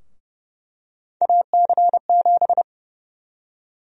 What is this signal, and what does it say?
Morse code, heard as a keyed tone, about 29 wpm, 700 Hz. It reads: AC7
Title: USA Call Sign Prefixes - 30wpm